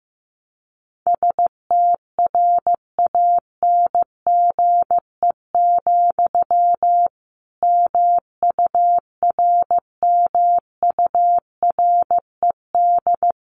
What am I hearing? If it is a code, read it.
Morse code, heard as a keyed tone, about 15 wpm, 700 Hz. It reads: STRANGE, MURMURED
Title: A Sound Of Thunder 15wpm